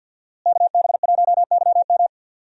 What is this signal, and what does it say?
Morse code, heard as a keyed tone, about 50 wpm, 700 Hz. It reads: KB1YM
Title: USA Call Signs - 5 Characters - Repeated 2x - 50wpm